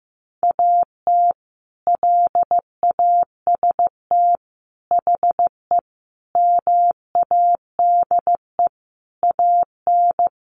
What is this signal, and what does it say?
Morse code, heard as a keyed tone, about 15 wpm, 700 Hz. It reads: AT LAST HE MADE AN
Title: Sentences from Top 1200 Words 15wpm